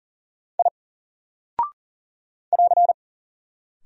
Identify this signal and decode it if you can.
Morse code, heard as a keyed tone, about 40 wpm, 700 Hz. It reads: I <AR>
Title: Single Character (Letter:Number:Punctuation:Prosign) 40wpm